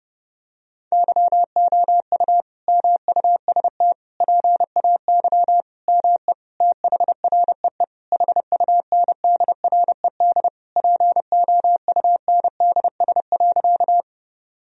Morse code, heard as a keyed tone, about 30 wpm, 700 Hz. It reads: YOU MUST PAY ME THREE HUNDRED POUNDS.
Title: Sentences from Top 1000 Words - Review Part 2 - 30wpm